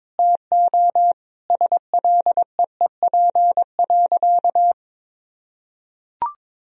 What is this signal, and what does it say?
Morse code, heard as a keyed tone, about 21 wpm, 700 Hz. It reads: TO SLEEP.
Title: Sentences from Top 1000 Words - Review Part 2 - 22wpm